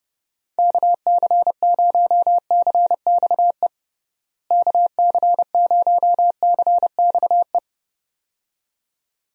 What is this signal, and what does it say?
Morse code, heard as a keyed tone, about 30 wpm, 700 Hz. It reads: KC0CXE KC0CXE
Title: USA Call Signs - 6 Characters - Repeated 2x - 30wpm